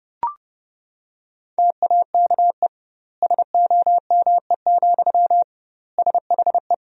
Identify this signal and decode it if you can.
Morse code, heard as a keyed tone, about 32 wpm, 700 Hz. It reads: TAKE SOME, SHE
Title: 2x Word-Spacing - Sentences from Top 200 Words - 30wpm